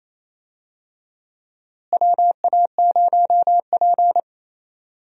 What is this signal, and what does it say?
Morse code, heard as a keyed tone, about 27 wpm, 700 Hz. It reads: WA0P
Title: USA Call Signs - 4 Characters - Repeated 2x - 28wpm